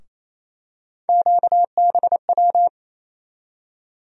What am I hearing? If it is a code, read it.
Morse code, heard as a keyed tone, about 27 wpm, 700 Hz. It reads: QBW
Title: USA Call Sign Postfixes - 3 Letters 28wpm